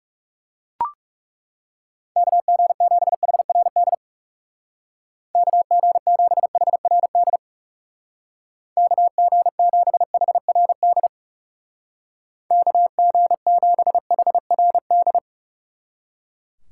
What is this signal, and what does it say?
Morse code, heard as a keyed tone, about 35 wpm, 700 Hz. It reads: KG7HRD KG7HRD KG7HRD KG7HRD
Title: Speed Racing - USA Call Signs 30wpm